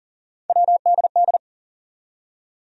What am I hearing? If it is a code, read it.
Morse code, heard as a keyed tone, about 40 wpm, 700 Hz. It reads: WDD
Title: USA Call Sign Postfixes - 3 Letters 40wpm